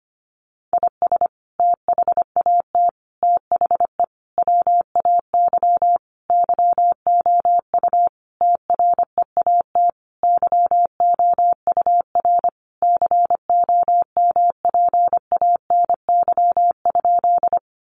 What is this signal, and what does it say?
Morse code, heard as a keyed tone, about 25 wpm, 700 Hz. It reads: IS THAT THE WAY YOU TREAT YOUR COMPANY?
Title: Sentences from Top 1300 Words 25wpm